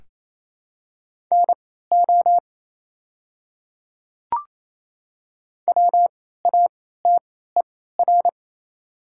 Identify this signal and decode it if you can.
Morse code, heard as a keyed tone, about 28 wpm, spaced as for 16 wpm, 700 Hz. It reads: NO WATER
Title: 3x Character Spacing - Top 100 Words - 28wpm